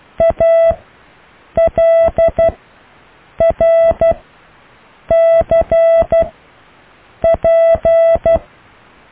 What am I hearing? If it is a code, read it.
Morse code, heard as a keyed tone, about 12 wpm, spaced as for 7 wpm, 660 Hz. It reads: ALRCP
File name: cw / Alle letters